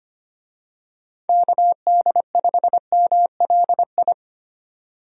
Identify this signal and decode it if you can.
Morse code, heard as a keyed tone, about 25 wpm, 700 Hz. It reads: KD5MLI
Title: USA Call Signs - 6 Characters - 25wpm